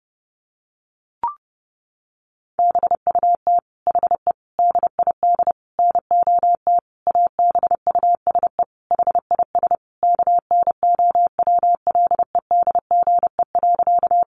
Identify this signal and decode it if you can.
Morse code, heard as a keyed tone, about 30 wpm, 700 Hz. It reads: BUT HE DID NOT ABUSE HIS KNOWLEDGE.
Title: Sentences from Top 1300 Words 30wpm